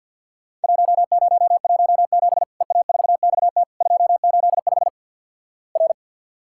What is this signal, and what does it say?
Morse code, heard as a keyed tone, about 50 wpm, 700 Hz. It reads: 1017 EA4XT 285 R
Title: Worked All Europe DX Contest - QTC Exchange - 50wpm